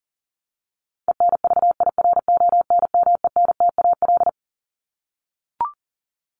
Morse code, heard as a keyed tone, about 40 wpm, 700 Hz. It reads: ENVIRONMENTAL
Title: Top 1001-1250 Words 40wpm